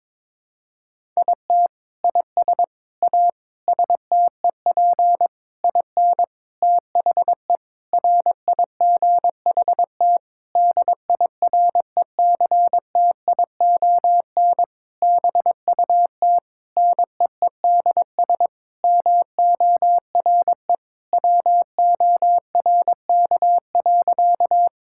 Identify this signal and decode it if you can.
Morse code, heard as a keyed tone, about 22 wpm, 700 Hz. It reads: IT IS A STEP IN THE RIGHT DIRECTION BUT NEEDS MORE WORK.
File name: practice / 22wpm / Sentences with 8-Word Phrases - 22wpm